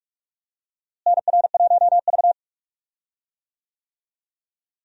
Morse code, heard as a keyed tone, about 46 wpm, 700 Hz. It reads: NR1V
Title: USA Call Signs - 4 Characters - Repeated 2x - 45wpm